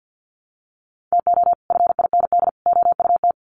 Morse code, heard as a keyed tone, about 50 wpm, 700 Hz. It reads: TO FIND OUT
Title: Sets of 3 Words 50wpm